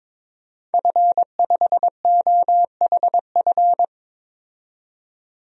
Morse code, heard as a keyed tone, about 22 wpm, 700 Hz. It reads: F5OHF
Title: DXCC-100 Call Signs - 5 Characters - 22wpm